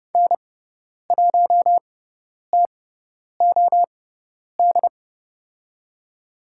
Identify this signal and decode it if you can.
Morse code, heard as a keyed tone, about 30 wpm, spaced as for 10 wpm, 700 Hz. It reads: N1TOD
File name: Lesson 16g - Letter c - Call Signs 30wpm